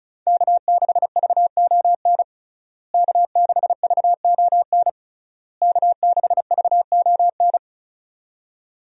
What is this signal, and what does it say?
Morse code, heard as a keyed tone, about 34 wpm, 700 Hz. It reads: K6VON K6VON K6VON
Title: USA Call Signs - 5 Characters - Repeated 3x - 35wpm